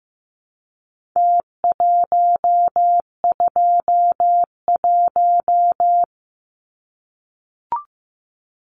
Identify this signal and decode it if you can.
Morse code, heard as a keyed tone, about 15 wpm, 700 Hz. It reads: T121
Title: Numbers - Contest Style - 4-Digit 15wpm